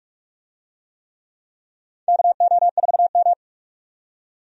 Morse code, heard as a keyed tone, about 46 wpm, 700 Hz. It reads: KO4M
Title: USA Call Signs - 4 Characters - 45wpm